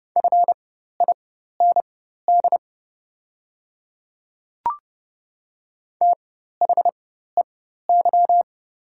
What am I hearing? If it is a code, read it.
Morse code, heard as a keyed tone, about 30 wpm, spaced as for 14 wpm, 690 Hz. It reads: FIND THEY
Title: 4x Character Spacing - Top 300 Words - 30wpm